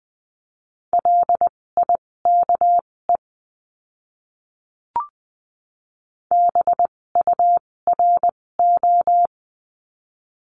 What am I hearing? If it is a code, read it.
Morse code, heard as a keyed tone, about 20 wpm, spaced as for 16 wpm, 700 Hz. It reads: LIKE BURO
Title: 256 Lesson 42a - Words 12-20wpm